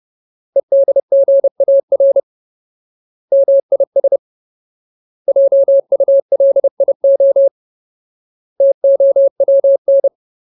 Morse code, heard as a keyed tone, about 30 wpm, 550 Hz. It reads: EDGAR MIS JULIO TOWN
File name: Ultimate Word List / 30WPM Ultimate